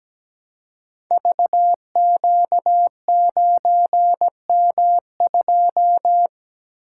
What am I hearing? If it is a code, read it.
Morse code, heard as a keyed tone, about 17 wpm, 700 Hz. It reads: VQ9M2
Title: Code Groups - 5 Letter:Numbers - 17wpm